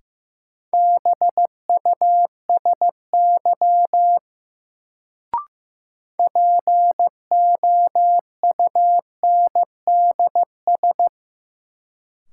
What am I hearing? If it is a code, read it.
Morse code, heard as a keyed tone, about 15 wpm, 700 Hz. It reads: BUSY POUNDS
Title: Sentences from Top 1000 Words - Review Part 2 - New Word Drill - 15wpm